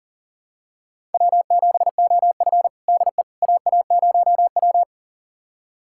Morse code, heard as a keyed tone, about 40 wpm, 700 Hz. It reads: W7OF DE AA0W
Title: USA Callsign DE Callsign - 4 Characters - 40wpm